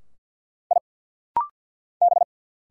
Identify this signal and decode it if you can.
Morse code, heard as a keyed tone, about 48 wpm, 700 Hz. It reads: I B
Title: Single Letter-Number Alt - Rapid-Fire 50wpm